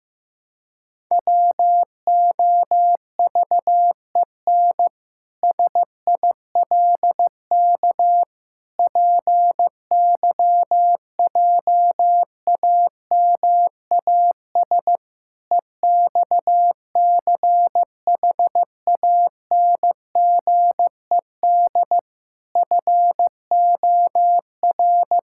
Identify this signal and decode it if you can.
Morse code, heard as a keyed tone, about 16 wpm, 700 Hz. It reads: WOVEN SILK PYJAMAS EXCHANGED FOR
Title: Pangram Challenge v2 - 15wpm